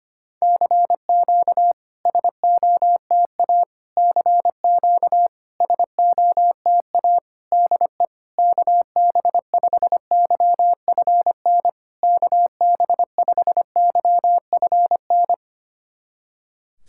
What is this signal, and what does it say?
Morse code, heard as a keyed tone, about 25 wpm, 700 Hz. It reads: CQ SOTA CQ SOTA DE KB5YFN KB5YFN
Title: SOTA QSO Practice 25wpm